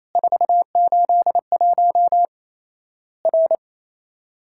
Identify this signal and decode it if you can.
Morse code, heard as a keyed tone, about 27 wpm, 700 Hz. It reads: 481 R
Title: Worked All Europe DX Contest - QTC Exchange - 28wpm